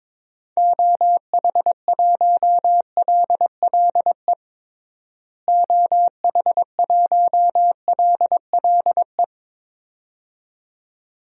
Morse code, heard as a keyed tone, about 22 wpm, 700 Hz. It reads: OH1LLE OH1LLE
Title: DXCC-100 Call Signs - 6 Characters - Repeated 2x - 22wpm